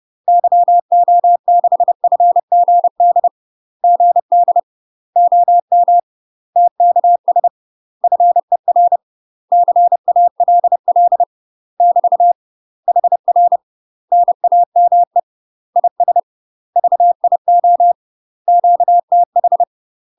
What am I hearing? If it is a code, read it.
Morse code, heard as a keyed tone, about 30 wpm, 700 Hz. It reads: YO6FGD GD OM TKS FER CALL <BT> HR NAME IS VIO QTH
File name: UK MASTER QSOs TRAINING / 30WPM / UK MASTER QSOs 0049